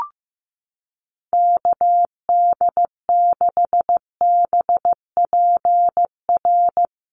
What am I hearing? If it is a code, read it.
Morse code, heard as a keyed tone, about 15 wpm, 700 Hz. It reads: KD6BPR
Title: USA Call Signs Repeated 2x 15wpm